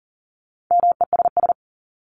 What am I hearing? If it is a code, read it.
Morse code, heard as a keyed tone, about 40 wpm, 700 Hz. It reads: MESS
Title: Top 1751-2000 Words 40wpm